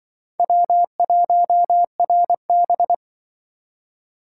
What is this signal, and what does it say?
Morse code, heard as a keyed tone, about 25 wpm, 700 Hz. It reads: W1RB
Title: Speed Racing - USA 1x2 Call Signs 20wpm